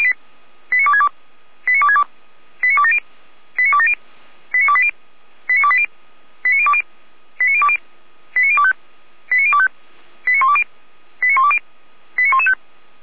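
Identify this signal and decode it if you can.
Five-tone selective call (ZVEI), heard as quick runs of five tones, tones of about 70 ms. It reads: 89262 89262 89270 89270 89280 89280 89020 89020 89025 89025 89120 89120 89115
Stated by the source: ZVEI five-tone call received by SDR.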